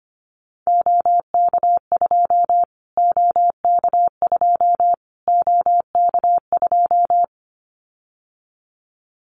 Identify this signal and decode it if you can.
Morse code, heard as a keyed tone, about 25 wpm, 700 Hz. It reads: OK2 OK2 OK2
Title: Callsign Prefixes for Top 25 DXCC - Repeated 3x - 25wpm